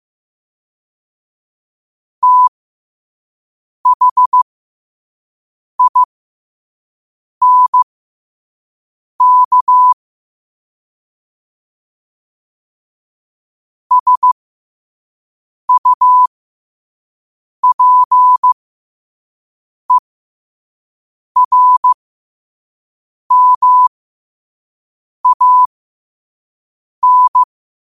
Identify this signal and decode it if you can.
Morse code, heard as a keyed tone, about 15 wpm, spaced as for 5 wpm, 1000 Hz. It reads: THINK SUPERMAN
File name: qotdfiles / 5wpm